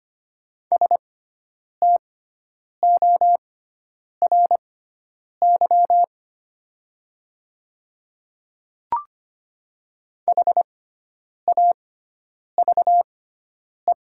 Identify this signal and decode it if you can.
Morse code, heard as a keyed tone, about 25 wpm, spaced as for 9 wpm, 700 Hz. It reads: STORY HAVE
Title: 6x Character Spacing - Top 300 Words - 25wpm